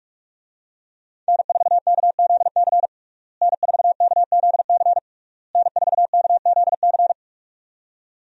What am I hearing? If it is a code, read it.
Morse code, heard as a keyed tone, about 46 wpm, 700 Hz. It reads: N4KZC N4KZC N4KZC
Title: USA Call Signs - 5 Characters - Repeated 3x - 45wpm